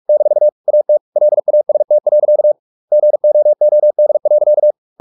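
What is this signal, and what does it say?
Morse code, heard as a keyed tone, about 44 wpm, 600 Hz. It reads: - AT LAST. GOOD.
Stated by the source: Morse practice at 45 wpm, the novel Never by Ken Follett in English, from CW med Gnister.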